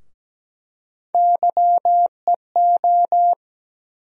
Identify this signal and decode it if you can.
Morse code, heard as a keyed tone, about 17 wpm, 700 Hz. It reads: YEO
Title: USA Call Sign Postfixes - 3 Letters 17wpm